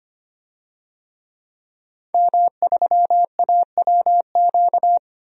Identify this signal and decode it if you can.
Morse code, heard as a keyed tone, about 25 wpm, 700 Hz. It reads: M3AWQ
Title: DXCC-100 Call Signs - 5 Characters - 25wpm